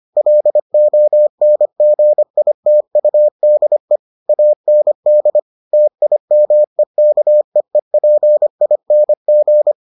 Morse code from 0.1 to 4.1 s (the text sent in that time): LONGITUDE